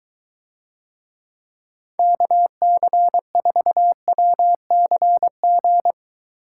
Morse code, heard as a keyed tone, about 23 wpm, 700 Hz. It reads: KC4WCG